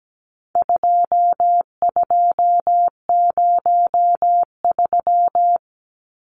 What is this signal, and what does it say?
Morse code, heard as a keyed tone, about 17 wpm, 700 Hz. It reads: 2203
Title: Numbers - 4-Digit 17wpm